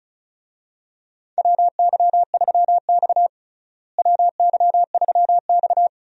Morse code, heard as a keyed tone, about 35 wpm, 700 Hz. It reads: WY3X WY3X
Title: USA Call Signs - 4 Characters - Repeated 2x - 35wpm